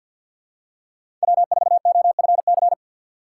Morse code, heard as a keyed tone, about 52 wpm, 700 Hz. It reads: W4OFC
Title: USA Call Signs - 5 Characters - 3x Character Spacing - 50wpm